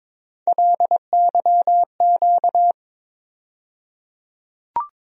Morse code, heard as a keyed tone, about 22 wpm, 700 Hz. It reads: LYQ
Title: USA Call Sign Postfixes - 3 Letters 22wpm